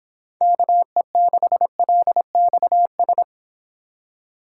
Morse code, heard as a keyed tone, about 31 wpm, 700 Hz. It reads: KE6LXS